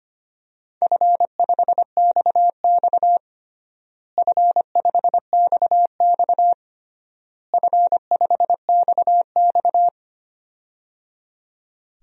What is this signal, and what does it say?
Morse code, heard as a keyed tone, about 25 wpm, 700 Hz. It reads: F5XX F5XX F5XX
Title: DXCC-100 Call Signs - 4 Characters - Repeated 3x - 25wpm